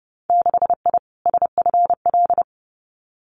Morse code, heard as a keyed tone, about 30 wpm, 700 Hz. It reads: 6I SFL
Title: Winter Field Day Exchanges 30wpm